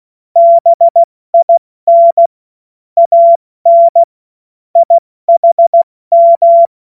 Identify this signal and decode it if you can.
Morse code, heard as a keyed tone, about 16 wpm, spaced as for 14 wpm, 680 Hz. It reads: BIN AN IHM
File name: Morsekurs / Witze / Witz 20. September 2022 70 F80